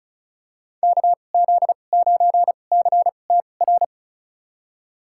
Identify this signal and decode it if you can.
Morse code, heard as a keyed tone, about 36 wpm, spaced as for 25 wpm, 700 Hz. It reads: KZ9CTR
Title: USA Call Signs - 6 Characters - 2x Character Spacing - 35wpm